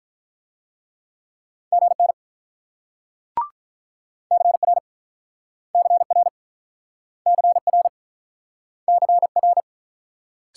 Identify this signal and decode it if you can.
Morse code, heard as a keyed tone, about 48 wpm, 690 Hz. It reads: GN CR CR CR CR